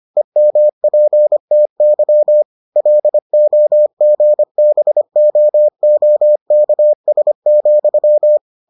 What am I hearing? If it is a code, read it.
Morse code, heard as a keyed tone, about 25 wpm, 600 Hz. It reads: EMPTY LOGBOOKS,